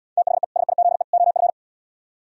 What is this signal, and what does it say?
Morse code, heard as a keyed tone, about 77 wpm, 700 Hz. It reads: THE SIZE OF